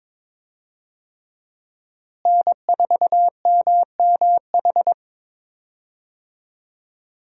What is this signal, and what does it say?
Morse code, heard as a keyed tone, about 22 wpm, 700 Hz. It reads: N4MMH
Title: USA Call Signs - 5 Characters - Repeated 3x - 22wpm